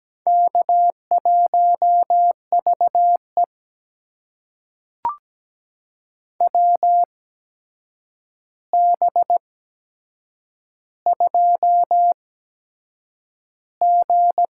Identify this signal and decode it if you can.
Morse code, heard as a keyed tone, about 17 wpm, spaced as for 7 wpm, 700 Hz. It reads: K1VE WB2G
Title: USA Call Signs - 8x Character Spacing - 17wpm